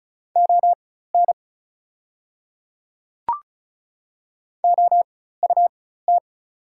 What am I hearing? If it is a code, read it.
Morse code, heard as a keyed tone, about 36 wpm, spaced as for 16 wpm, 690 Hz. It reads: ON OUT
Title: 4x Character Spacing - Top 100 Words - 35wpm